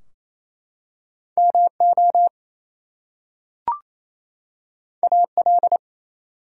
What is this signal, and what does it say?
Morse code, heard as a keyed tone, about 27 wpm, 700 Hz. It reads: MO AL
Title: US States and Canadian Provinces 28wpm